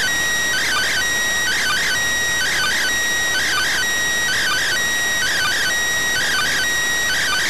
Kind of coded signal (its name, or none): two-tone teleprinter signal